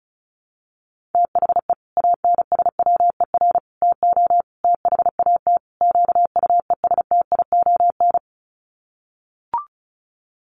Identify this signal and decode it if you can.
Morse code, heard as a keyed tone, about 34 wpm, 700 Hz. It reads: THE ANSWER TO THAT QUESTION
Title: Sets of 5 Words Encore2 35wpm